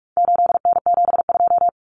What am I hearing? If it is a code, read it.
Morse code, heard as a keyed tone, about 44 wpm, 700 Hz. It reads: 8N72
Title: Common Cut Numbers - 4-Digit 45wpm